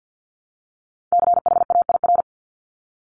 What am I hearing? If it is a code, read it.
Morse code, heard as a keyed tone, about 50 wpm, 700 Hz. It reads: CHAIR
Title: Multi-letter Phonemes Repeated 3x 50wpm